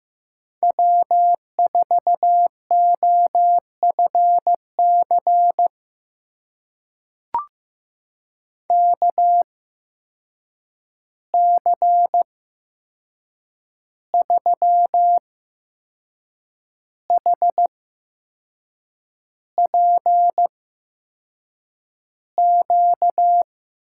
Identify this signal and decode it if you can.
Morse code, heard as a keyed tone, about 15 wpm, spaced as for 6 wpm, 700 Hz. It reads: W4OFC KC3HPQ
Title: USA Call Signs - 8x Character Spacing - 15wpm